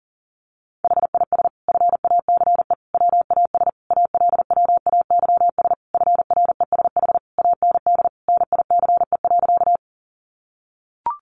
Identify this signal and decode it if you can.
Morse code, heard as a keyed tone, about 40 wpm, 700 Hz. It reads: HIS FACE WAS ALWAYS FRESH AND NICE.
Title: Sentences from Top 1200 Words 40wpm